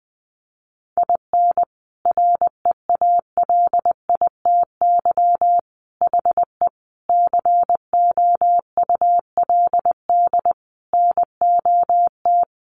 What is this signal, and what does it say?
Morse code, heard as a keyed tone, about 20 wpm, 700 Hz. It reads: IN REALITY HE COULD NOT
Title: Sentences from Top 1200 Words 20wpm